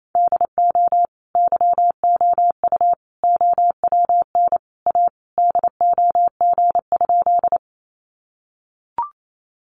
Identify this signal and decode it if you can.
Morse code, heard as a keyed tone, about 28 wpm, 700 Hz. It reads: DO YOU OWN A DOG?
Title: Sentences from Top 400 Words 28wpm